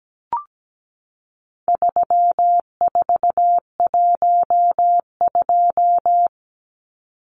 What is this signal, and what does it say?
Morse code, heard as a keyed tone, about 17 wpm, 700 Hz. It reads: 3412
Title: Numbers - 4-Digit 17wpm